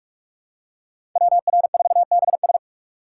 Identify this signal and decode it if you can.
Morse code, heard as a keyed tone, about 46 wpm, 700 Hz. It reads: WR4BS
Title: USA Call Signs - 5 Characters - 4x Character Spacing - 45wpm